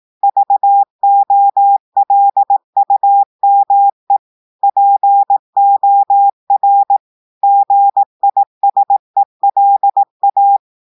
Morse code, heard as a keyed tone, about 18 wpm, 800 Hz. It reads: VOLUME POR GISELA
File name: audios / portugais / portugais texte en clair 1 vitesse 18 mots minutes